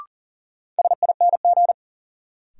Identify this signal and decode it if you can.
Morse code, heard as a keyed tone, about 39 wpm, 700 Hz. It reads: SING
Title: Sentences from Top 400 Words - New Word Drill - 40wpm